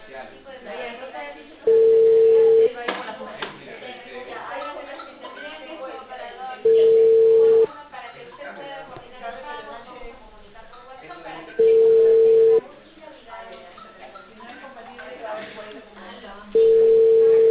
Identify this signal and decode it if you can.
Ringback tone (China, 450 Hz), with speech.